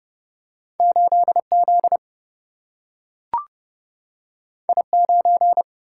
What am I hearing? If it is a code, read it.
Morse code, heard as a keyed tone, about 30 wpm, 700 Hz. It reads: 8 Z I 9
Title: Single-Letter Single-Number 30wpm